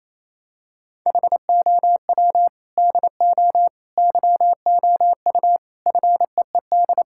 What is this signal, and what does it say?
Morse code, heard as a keyed tone, about 28 wpm, 700 Hz. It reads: HOW DO YOU FEED
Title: Sentences from Top 1600 Words 28wpm